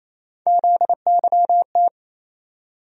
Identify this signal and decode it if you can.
Morse code, heard as a keyed tone, about 28 wpm, 700 Hz. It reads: ZYT
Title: USA Call Sign Postfixes - 3 Letters 28wpm